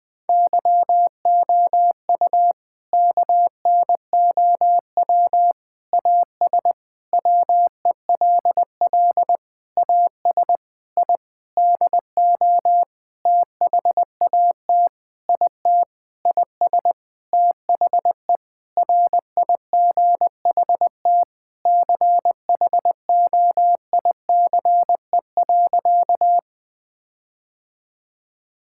Morse code, heard as a keyed tone, about 20 wpm, 700 Hz. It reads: YOU KNOW AS WELL AS I DO THAT IT IS THE RIGHT CHOICE.
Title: Sentences with 8-Word Phrases - 20wpm